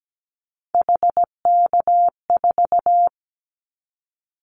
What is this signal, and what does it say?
Morse code, heard as a keyed tone, about 17 wpm, 700 Hz. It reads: HK4
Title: Callsign Prefixes for Top 100 DXCC - 17wpm